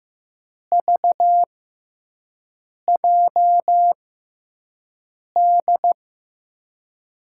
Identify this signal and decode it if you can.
Morse code, heard as a keyed tone, about 16 wpm, spaced as for 5 wpm, 700 Hz. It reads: VJD
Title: USA Call Signs - 6 Characters - 6x Character Spacing - 15wpm